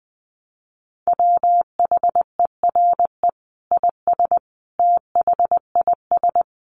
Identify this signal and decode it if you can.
Morse code, heard as a keyed tone, about 20 wpm, 700 Hz. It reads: WHERE IS THIS
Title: Sentences from Top 300 Words 20wpm